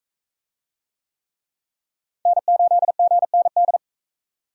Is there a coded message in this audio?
Yes: Morse code from 2.3 to 3.8 s, heard as a keyed tone, about 41 wpm, 700 Hz. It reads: N8GND